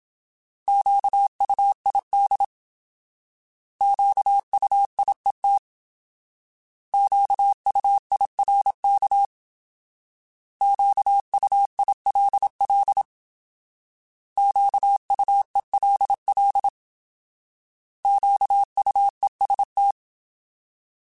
Morse code, heard as a keyed tone, about 27 wpm, 800 Hz. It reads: QUID QUIET QUIRK QUILL QUELL QUEST